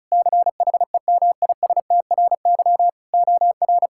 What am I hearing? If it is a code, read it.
Morse code, heard as a keyed tone, about 36 wpm, 700 Hz. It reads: CHEMISTRY OR
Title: Sentences from Top 1000 Words - Review Part 6 - 35wpm